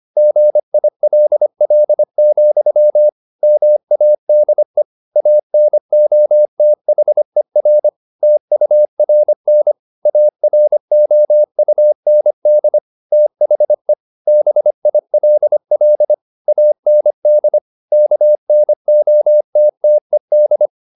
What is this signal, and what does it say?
Morse code, heard as a keyed tone, about 25 wpm, 600 Hz. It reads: GILL, MADE ANOTHER TURN AROUND THE BILL AND KNOTTED